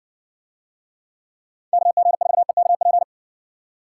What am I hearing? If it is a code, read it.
Morse code, heard as a keyed tone, about 60 wpm, 700 Hz. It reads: KG4EKP